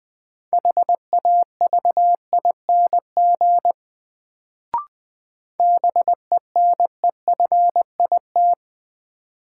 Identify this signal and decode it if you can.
Morse code, heard as a keyed tone, about 20 wpm, 700 Hz. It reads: HAVING BENEFIT
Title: Sets of 5 Words Encore - New Word Drill - 20wpm